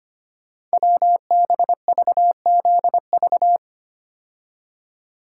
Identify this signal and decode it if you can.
Morse code, heard as a keyed tone, about 25 wpm, 700 Hz. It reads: WBVZV
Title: Code Groups - 5 Letters - 25wpm